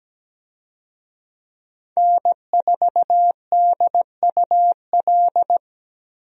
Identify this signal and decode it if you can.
Morse code, heard as a keyed tone, about 17 wpm, 700 Hz. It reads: N4DUL
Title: USA Call Signs - 5 Characters - Repeated 3x - 17wpm